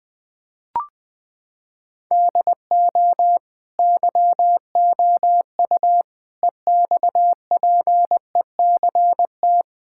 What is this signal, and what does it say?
Morse code, heard as a keyed tone, about 21 wpm, 700 Hz. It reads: DO YOU EXPECT
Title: Sentences from Top 1600 Words 20wpm